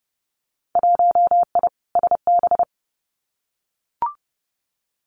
Morse code, heard as a keyed tone, about 30 wpm, 700 Hz. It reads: 1I SB
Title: Winter Field Day Exchanges Encore 30wpm